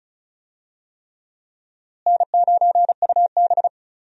Morse code, heard as a keyed tone, about 36 wpm, 700 Hz. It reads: N9UB
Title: USA Call Signs - 4 Characters - Repeated 3x - 35wpm